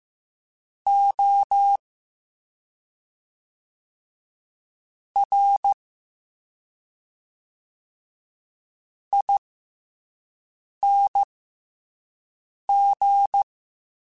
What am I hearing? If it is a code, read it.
Morse code, heard as a keyed tone, about 15 wpm, spaced as for 5 wpm, 800 Hz. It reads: O R ING